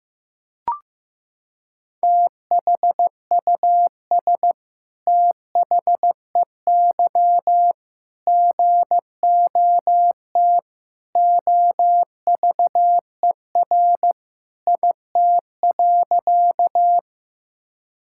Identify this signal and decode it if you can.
Morse code, heard as a keyed tone, about 15 wpm, 700 Hz. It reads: THUS THEY GOT OVER IT.
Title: Sentences from Top 1000 Words - Review Part 5 - 15wpm